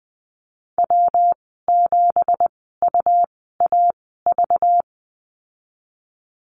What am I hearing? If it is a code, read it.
Morse code, heard as a keyed tone, about 19 wpm, spaced as for 14 wpm, 700 Hz. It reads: W7UAV
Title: 193 Lesson 31g - Letter v - Call Signs 20wpm